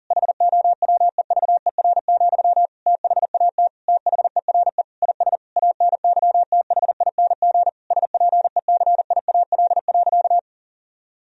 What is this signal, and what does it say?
Morse code, heard as a keyed tone, about 40 wpm, 700 Hz. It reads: HOWEVER, THAT THERE IS ANYTHING SPECIAL.
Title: Sentences from Top 1000 Words - Review Part 3 - 40wpm